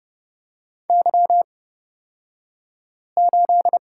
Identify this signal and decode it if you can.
Morse code, heard as a keyed tone, about 30 wpm, spaced as for 5 wpm, 700 Hz. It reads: Y8
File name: Lesson 37d - Letter x - Two Characters 30wpm